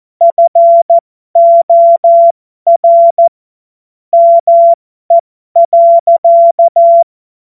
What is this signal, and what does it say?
Morse code, heard as a keyed tone, about 14 wpm, spaced as for 12 wpm, 680 Hz. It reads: FOR ME.
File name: Morsekurs / Witze